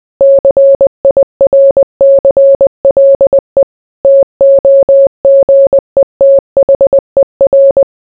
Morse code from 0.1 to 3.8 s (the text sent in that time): CIRCLE